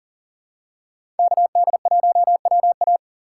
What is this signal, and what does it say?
Morse code, heard as a keyed tone, about 40 wpm, 700 Hz. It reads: KD1WA
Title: USA Call Signs - 5 Characters - 4x Character Spacing - 40wpm